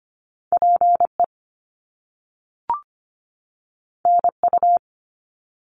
Morse code, heard as a keyed tone, about 25 wpm, 700 Hz. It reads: PE NU
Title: Canadian Provinces and Territories 25wpm